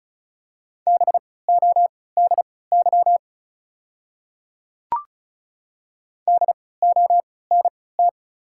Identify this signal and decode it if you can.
Morse code, heard as a keyed tone, about 36 wpm, spaced as for 20 wpm, 690 Hz. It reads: BODY DONT
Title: 3x Character Spacing - Top 500 Words - 35wpm